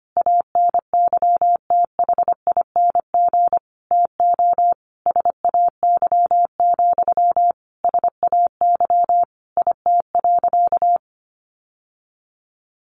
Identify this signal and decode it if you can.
Morse code, heard as a keyed tone, about 25 wpm, 700 Hz. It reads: ANYTHING TO SAY, SAY IT.
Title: Sentences from Top 1000 Words 25wpm